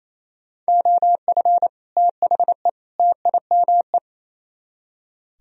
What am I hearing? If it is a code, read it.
Morse code, heard as a keyed tone, about 35 wpm, 700 Hz. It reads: OF THE TIME
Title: Speed-Racing - Sets of 3 Words 28wpm